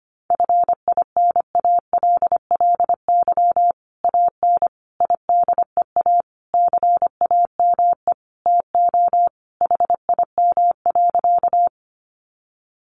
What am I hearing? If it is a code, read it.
Morse code, heard as a keyed tone, about 25 wpm, 700 Hz. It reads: FINALLY AN IDEA CAME TO HIM.
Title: Sentences from Top 1000 Words 25wpm